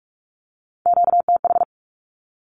Morse code, heard as a keyed tone, about 44 wpm, 700 Hz. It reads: QTH
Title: Q-Codes Repeated 3x 45wpm